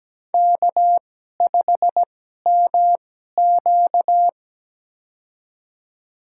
Morse code, heard as a keyed tone, about 17 wpm, spaced as for 12 wpm, 700 Hz. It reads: K5MQ
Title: USA Call Signs - 2x Character Spacing - 17wpm